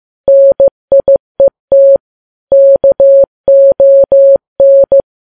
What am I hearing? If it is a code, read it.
Morse code, heard as a keyed tone, about 15 wpm, 550 Hz. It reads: NIET KON
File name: Files Web tls / Morse on9fv / morse15kps